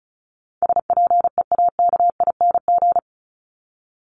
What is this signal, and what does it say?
Morse code, heard as a keyed tone, about 34 wpm, 700 Hz. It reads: SPEAKING
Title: ING Suffix - Top 100 Words with Base Words 35wpm